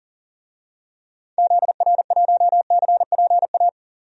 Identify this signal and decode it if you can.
Morse code, heard as a keyed tone, about 40 wpm, 700 Hz. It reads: ZR1CPA
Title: DXCC-100 Call Signs - 6 Characters - 40wpm